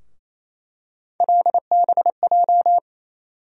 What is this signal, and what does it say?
Morse code, heard as a keyed tone, about 28 wpm, 700 Hz. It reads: LBJ
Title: USA Call Sign Postfixes - 3 Letters 28wpm